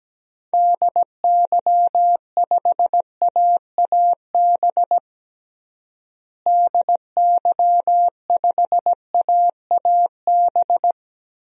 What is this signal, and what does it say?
Morse code, heard as a keyed tone, about 17 wpm, 700 Hz. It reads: DY5AAB DY5AAB
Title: DXCC-100 Call Signs - 6 Characters - Repeated 2x - 17wpm